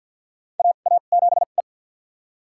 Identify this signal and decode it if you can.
Morse code, heard as a keyed tone, about 53 wpm, spaced as for 37 wpm, 700 Hz. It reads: AA7E
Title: USA Call Signs - 4 Characters - 2x Character Spacing - 50wpm